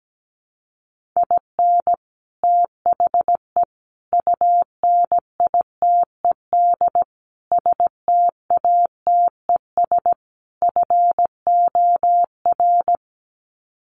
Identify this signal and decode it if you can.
Morse code, heard as a keyed tone, about 17 wpm, 700 Hz. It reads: IN THE UNITED STATES FOR
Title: Sets of 5 Words Encore2 17wpm